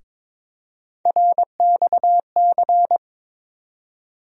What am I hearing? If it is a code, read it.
Morse code, heard as a keyed tone, about 22 wpm, 700 Hz. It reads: RXC
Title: USA Call Sign Postfixes - 3 Letters 22wpm